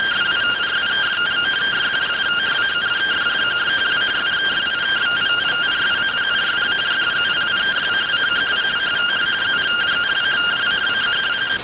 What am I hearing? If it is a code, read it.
Two-tone teleprinter signal, about 170 Hz shift.